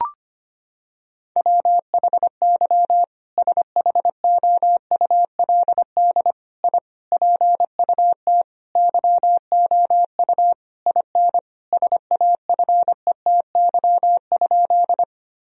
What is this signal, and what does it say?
Morse code, heard as a keyed tone, about 25 wpm, 700 Hz. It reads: WHY SHOULD I PUT YOU IN SAFETY?
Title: Sentences from Top 1000 Words - Review Part 4 - 25wpm